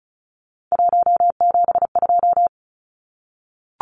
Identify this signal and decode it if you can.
Morse code, heard as a keyed tone, about 34 wpm, 700 Hz. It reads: 172
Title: Numbers - 3-Digit 35wpm